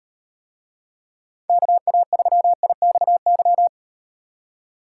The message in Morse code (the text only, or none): KA3IXY